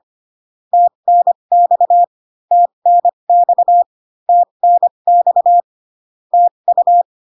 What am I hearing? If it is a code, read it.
Morse code, heard as a keyed tone, about 25 wpm, spaced as for 22 wpm, 700 Hz. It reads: TNX TNX TNX TU
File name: CW common abbreviations alphabetical repeated 3 times 25 18wpm